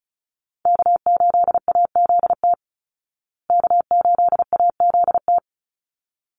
Morse code, heard as a keyed tone, about 34 wpm, 700 Hz. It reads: K8AZT K8AZT
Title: Call Signs Repeated 3x - 35wpm